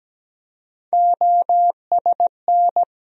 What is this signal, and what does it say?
Morse code, heard as a keyed tone, about 17 wpm, 700 Hz. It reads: OSN
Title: USA Call Sign Postfixes - 3 Letters 17wpm